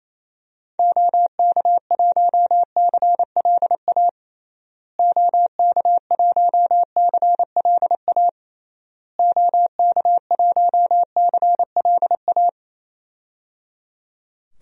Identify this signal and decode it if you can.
Morse code, heard as a keyed tone, about 28 wpm, 700 Hz. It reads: OK1CLA OK1CLA OK1CLA
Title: DXCC-100 Call Signs - 6 Characters - Repeated 3x - 28wpm